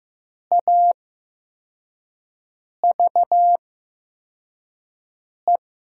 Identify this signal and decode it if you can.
Morse code, heard as a keyed tone, about 15 wpm, spaced as for 4 wpm, 700 Hz. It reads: AVE
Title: 8x Character Spacing - Top 100 Words - 15wpm